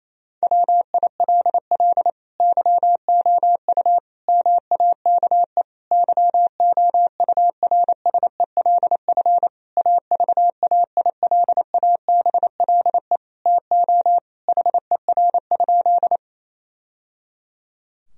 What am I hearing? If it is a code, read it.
Morse code, heard as a keyed tone, about 28 wpm, 700 Hz. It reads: WILL YOU MAKE YOURSELF AVAILABLE TO HER?
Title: Sentences from Top 1000 Words - Review Part 3 - 28wpm